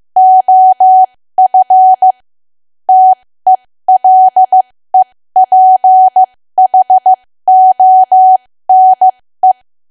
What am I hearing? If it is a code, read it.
Morse code, heard as a keyed tone, about 15 wpm, spaced as for 13 wpm, 750 Hz. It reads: OF TELEPHONE
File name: cwa / W1AW-Level2 / 13WPM